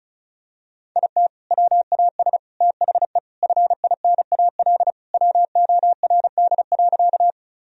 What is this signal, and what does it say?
Morse code, heard as a keyed tone, about 36 wpm, 700 Hz. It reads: IT WAS THE FINAL WORD.
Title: Sentences from Top 1000 Words - Review Part 2 - 35wpm